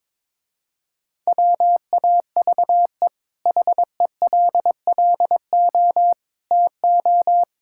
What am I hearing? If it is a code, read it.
Morse code, heard as a keyed tone, about 22 wpm, 700 Hz. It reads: WAVE HELLO TO
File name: practice / 22wpm / Easy Sentences with Hints - 22wpm